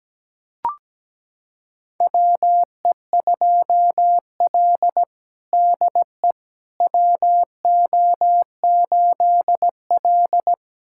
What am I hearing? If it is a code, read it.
Morse code, heard as a keyed tone, about 17 wpm, 700 Hz. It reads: WE2L DE WO8L
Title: USA Callsign DE Callsign - 4 Characters - 17wpm